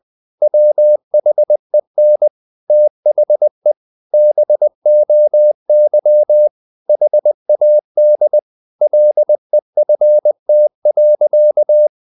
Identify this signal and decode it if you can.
Morse code, heard as a keyed tone, about 20 wpm, 600 Hz. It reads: WHEN THE BOY HAD LEFT.
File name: the old man and the sea / the old man and the sea part 1 20wpm